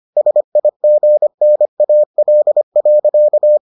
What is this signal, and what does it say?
Morse code, heard as a keyed tone, about 25 wpm, 600 Hz. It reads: SIGNAL.